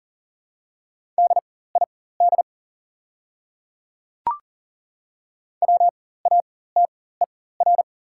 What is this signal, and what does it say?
Morse code, heard as a keyed tone, about 40 wpm, spaced as for 19 wpm, 700 Hz. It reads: DID WATER
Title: 4x Character Spacing - Top 100 Words - 40wpm